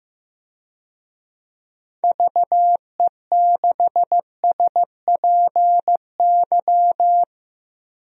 Morse code, heard as a keyed tone, about 15 wpm, 700 Hz. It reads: VE6SPY
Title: DXCC-100 Call Signs - 6 Characters - Repeated 2x - 15wpm